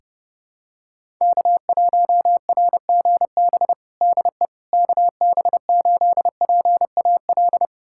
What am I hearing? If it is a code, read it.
Morse code, heard as a keyed tone, about 30 wpm, 700 Hz. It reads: K1RGB DE KB8PAL
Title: USA Callsign DE Callsign 30wpm